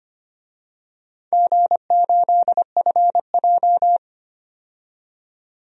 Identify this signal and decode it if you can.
Morse code, heard as a keyed tone, about 25 wpm, 700 Hz. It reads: G8FJ
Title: DXCC-100 Call Signs - 4 Characters - 25wpm